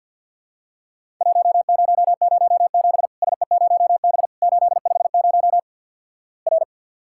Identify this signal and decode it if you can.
Morse code, heard as a keyed tone, about 53 wpm, 700 Hz. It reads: 1007 SE0B 850 R
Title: Worked All Europe DX Contest - QTC Exchange - 50wpm